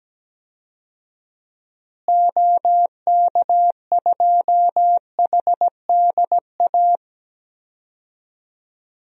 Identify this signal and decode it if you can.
Morse code, heard as a keyed tone, about 17 wpm, 700 Hz. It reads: OK2HDA
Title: DXCC-100 Call Signs - 6 Characters - Repeated 3x - 17wpm